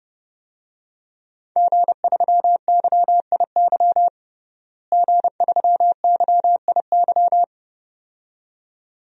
Morse code, heard as a keyed tone, about 30 wpm, 700 Hz. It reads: G3YIY G3YIY
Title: DXCC-100 Call Signs - 5 Characters - Repeated 2x - 30wpm